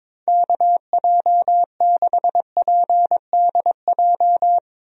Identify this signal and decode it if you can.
Morse code, heard as a keyed tone, about 22 wpm, 700 Hz. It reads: KJ6PDJ
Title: USA Call Signs - 6 Characters - Repeated 2x - 22wpm